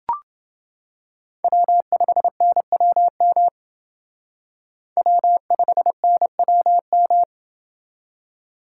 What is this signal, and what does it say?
Morse code, heard as a keyed tone, about 30 wpm, 700 Hz. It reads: W5NWM W5NWM